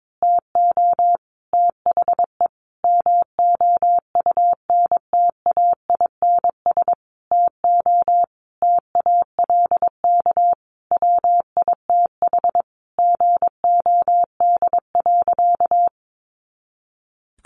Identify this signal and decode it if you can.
Morse code, heard as a keyed tone, about 22 wpm, 700 Hz. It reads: TO THE MOUNTAINS TO TALK WITH GOD.
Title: A Sound Of Thunder 22wpm